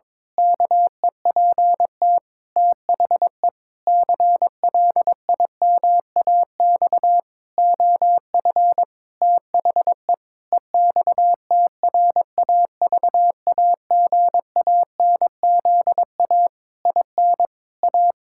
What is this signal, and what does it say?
Morse code, heard as a keyed tone, about 22 wpm, 700 Hz. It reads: KEPT THE CLIMAX OF THE EXTRAVAGANZA IN A
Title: Pangram Challenge v2 - 22wpm